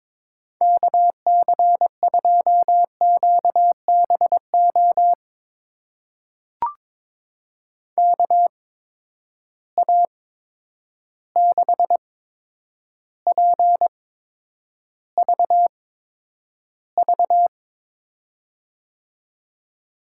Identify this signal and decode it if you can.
Morse code, heard as a keyed tone, about 22 wpm, spaced as for 9 wpm, 700 Hz. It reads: KC2QBO KA6PVV
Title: USA Call Signs - 6 Characters - 8x Character Spacing - 22wpm